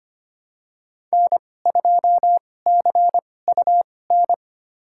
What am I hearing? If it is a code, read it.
Morse code, heard as a keyed tone, about 25 wpm, spaced as for 18 wpm, 700 Hz. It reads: N2CUN
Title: USA Call Signs - 5 Characters - 2x Character Spacing - 25wpm